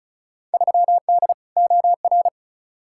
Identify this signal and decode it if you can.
Morse code, heard as a keyed tone, about 36 wpm, 700 Hz. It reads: 3D OR
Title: Field Day Exchanges Encore - 35wpm